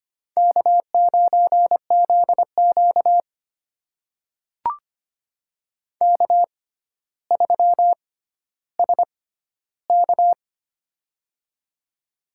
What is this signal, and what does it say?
Morse code, heard as a keyed tone, about 25 wpm, spaced as for 13 wpm, 700 Hz. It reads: K9ZQ K3SK
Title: USA Call Signs - 4 Characters - 6x Character Spacing - 25wpm